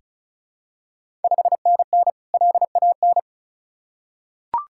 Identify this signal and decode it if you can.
Morse code, heard as a keyed tone, about 36 wpm, 700 Hz. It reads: 5NN LAN
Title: Ontario QSO Party Exchange 35wpm